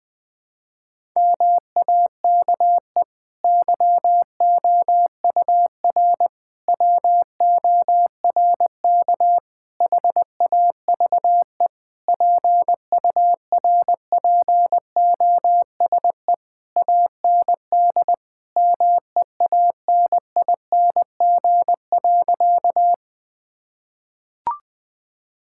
Morse code, heard as a keyed tone, about 20 wpm, 700 Hz. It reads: MAKE YOUR WORK HAVE PURPOSE AND MEANING.
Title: Sentences from Top 1000 Words - Review Part 6 - 20wpm